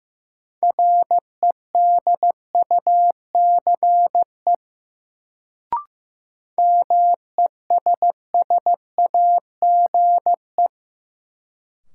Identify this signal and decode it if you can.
Morse code, heard as a keyed tone, about 15 wpm, 700 Hz. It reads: REDUCE MESSAGE
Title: Sentences from Top 1100 Words - New Word Drill - 15wpm